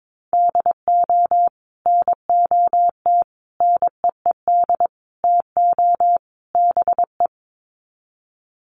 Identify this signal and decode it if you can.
Morse code, heard as a keyed tone, about 22 wpm, 700 Hz. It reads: DO NOT NEED TO BE
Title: Sets of 5 Words Encore 22wpm